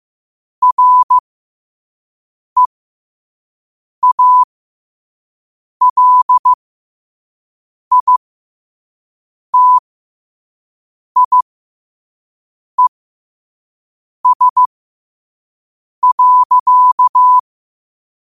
Morse code, heard as a keyed tone, about 15 wpm, spaced as for 5 wpm, 1000 Hz. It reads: REALITIES.
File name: 2024-10-07-5wpm